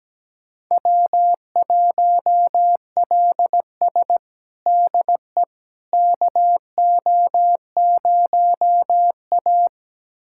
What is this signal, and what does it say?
Morse code, heard as a keyed tone, about 17 wpm, 700 Hz. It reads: W1LS DE KO0A
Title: USA Callsign DE Callsign - 4 Characters - 17wpm